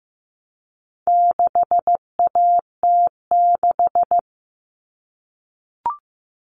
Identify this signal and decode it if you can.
Morse code, heard as a keyed tone, about 15 wpm, 700 Hz. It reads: 6AT6
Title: All Cut Numbers - 4-Digit 15wpm